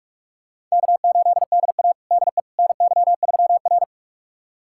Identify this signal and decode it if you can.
Morse code, heard as a keyed tone, about 44 wpm, 690 Hz. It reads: K8DA DE NY3R
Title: USA Callsign DE Callsign - 4 Characters - 45wpm